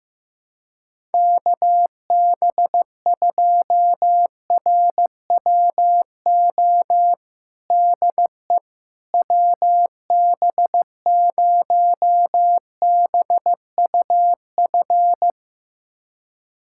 Morse code, heard as a keyed tone, about 15 wpm, 700 Hz. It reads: KB2RWO DE WB0BUF
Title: USA Callsign DE Callsign - 6 Characters - 15wpm